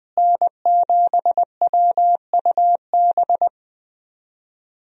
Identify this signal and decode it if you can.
Morse code, heard as a keyed tone, about 20 wpm, 700 Hz. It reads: N7WUB
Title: USA Call Signs - 5 Characters - Repeated 2x - 20wpm